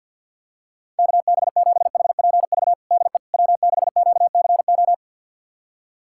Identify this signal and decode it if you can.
Morse code, heard as a keyed tone, about 52 wpm, 700 Hz. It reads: KB7HPV DE W6QCO
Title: USA Callsign DE Callsign 50wpm